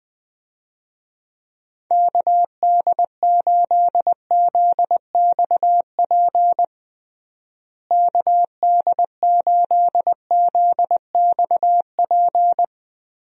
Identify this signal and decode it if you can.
Morse code, heard as a keyed tone, about 20 wpm, 700 Hz. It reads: KD8ZXP KD8ZXP
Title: USA Call Signs - 6 Characters - Repeated 3x - 20wpm